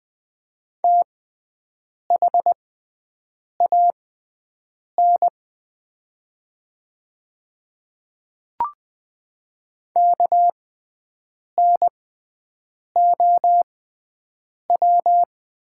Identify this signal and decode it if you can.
Morse code, heard as a keyed tone, about 20 wpm, spaced as for 7 wpm, 700 Hz. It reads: THAN KNOW
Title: 6x Character Spacing - Top 500 Words - 20wpm